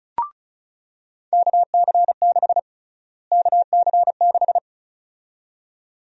Morse code, heard as a keyed tone, about 36 wpm, 700 Hz. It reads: KC6 KC6
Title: USA Call Sign Prefixes - Repeated 2x - 35wpm